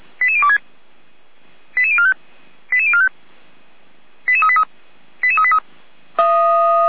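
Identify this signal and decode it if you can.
Five-tone selective call (ZVEI), heard as quick runs of five tones, tones of about 70 ms. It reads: 88026 88045 88045 88383 88383
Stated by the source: Five-tone ZVEI selective call received by SDR, logged by the fire brigade Feuerwehr Orlamünde.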